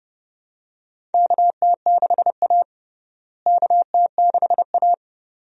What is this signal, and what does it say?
Morse code, heard as a keyed tone, about 32 wpm, 700 Hz. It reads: KT6A KT6A
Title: USA Call Signs - 4 Characters - Repeated 2x - 30wpm